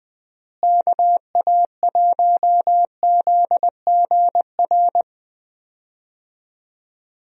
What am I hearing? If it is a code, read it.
Morse code, heard as a keyed tone, about 21 wpm, 700 Hz. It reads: KA1ZGR
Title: USA Call Signs - 6 Characters - Repeated 3x - 20wpm